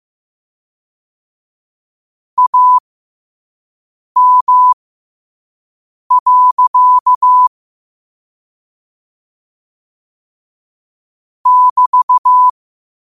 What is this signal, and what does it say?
Morse code, heard as a keyed tone, about 15 wpm, spaced as for 5 wpm, 1000 Hz. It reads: AM. <BT>
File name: qotdfiles / 5wpm